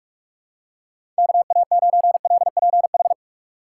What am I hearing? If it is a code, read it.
Morse code, heard as a keyed tone, about 46 wpm, 700 Hz. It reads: KA9LPH
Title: USA Call Signs - 4x Character Spacing - 45wpm